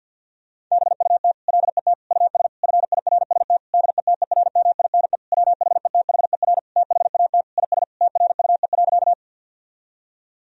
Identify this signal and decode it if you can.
Morse code, heard as a keyed tone, about 50 wpm, 700 Hz. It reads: BUT LET US FIRST DETERMINE WHETHER THAT IS TRUE.
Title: Sentences from Top 1000 Words - Review Part 4 - 50wpm